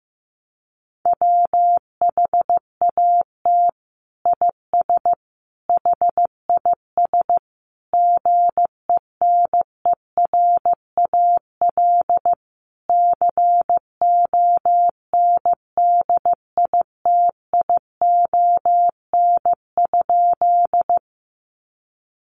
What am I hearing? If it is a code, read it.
Morse code, heard as a keyed tone, about 14 wpm, 700 Hz. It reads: WHAT IS HIS GENERAL CONDITION?
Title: Sentences from Top 1000 Words 15wpm